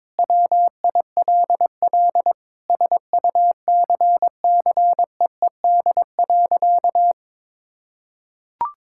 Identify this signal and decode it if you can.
Morse code, heard as a keyed tone, about 22 wpm, 700 Hz. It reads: WILL SUCCEED.
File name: Sentences with 9-Word Phrases - 22wpm